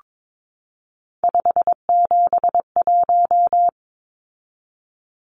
Morse code, heard as a keyed tone, about 22 wpm, 700 Hz. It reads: 571
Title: Numbers - 3-Digit 22wpm